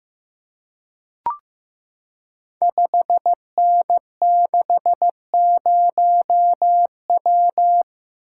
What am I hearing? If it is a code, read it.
Morse code, heard as a keyed tone, about 15 wpm, 700 Hz. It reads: 5N60W
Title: Code Groups - 5 Letter:Numbers - 15wpm